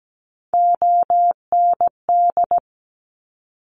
Morse code, heard as a keyed tone, about 17 wpm, 700 Hz. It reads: OND
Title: Top Three Letter Combinations in Words 17wpm